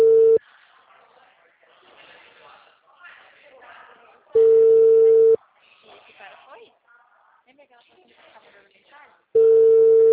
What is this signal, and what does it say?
Ringback tone (China, 450 Hz), with speech.